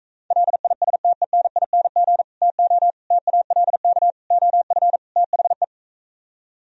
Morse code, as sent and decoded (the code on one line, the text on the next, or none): .-.. .. ... - . -. .. -. --. / - --- / - .- .-.. -.- / --- ..-. / - .... .
LISTENING TO TALK OF THE